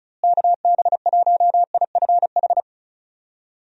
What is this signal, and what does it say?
Morse code, heard as a keyed tone, about 34 wpm, 700 Hz. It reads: KB1IFH
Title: USA Call Signs - 6 Characters - Repeated 3x - 35wpm